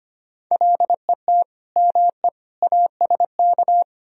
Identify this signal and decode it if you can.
Morse code, heard as a keyed tone, about 33 wpm, 700 Hz. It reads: LET ME ASK